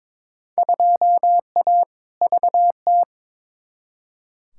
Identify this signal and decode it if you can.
Morse code, heard as a keyed tone, about 22 wpm, 700 Hz. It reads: 2A VT
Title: Field Day Exchanges Encore - 22wpm